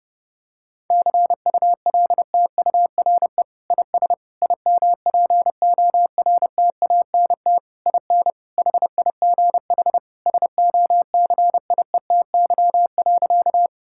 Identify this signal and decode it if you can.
Morse code, heard as a keyed tone, about 30 wpm, 700 Hz. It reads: CULTURE IS IMPORTANT IN HIGH SOCIETY.
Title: Sentences from Top 1000 Words - Review Part 6 - 30wpm